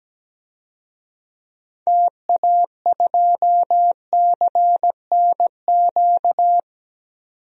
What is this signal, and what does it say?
Morse code, heard as a keyed tone, about 17 wpm, 700 Hz. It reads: TA2CNQ
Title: DXCC-100 Call Signs - 6 Characters - 17wpm